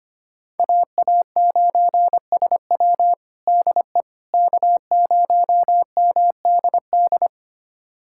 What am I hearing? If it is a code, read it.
Morse code, heard as a keyed tone, about 25 wpm, 700 Hz. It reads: AA9SW DE K0MDD
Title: USA Callsign DE Callsign 25wpm